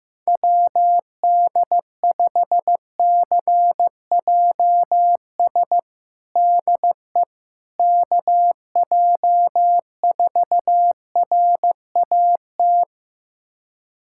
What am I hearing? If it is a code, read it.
Morse code, heard as a keyed tone, about 15 wpm, 700 Hz. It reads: WD5CJS DE KJ4RAT
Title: USA Callsign DE Callsign 15wpm